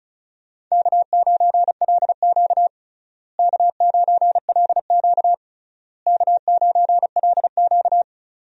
Morse code, heard as a keyed tone, about 34 wpm, 700 Hz. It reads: K9LQ K9LQ K9LQ
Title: USA Call Signs - 4 Characters - Repeated 3x - 35wpm